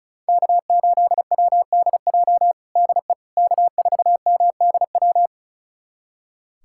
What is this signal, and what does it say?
Morse code, heard as a keyed tone, about 34 wpm, 700 Hz. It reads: K8WDJ DE K4MDW
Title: USA Callsign DE Callsign - 5 Characters - 35wpm